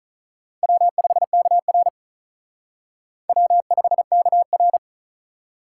Morse code, heard as a keyed tone, about 45 wpm, 700 Hz. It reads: W5KR W5KR